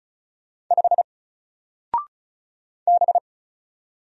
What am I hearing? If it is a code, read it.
Morse code, heard as a keyed tone, about 34 wpm, 700 Hz. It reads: 5 B
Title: Single Character (Letter:Number:Punctuation:Prosign) Alt 35wpm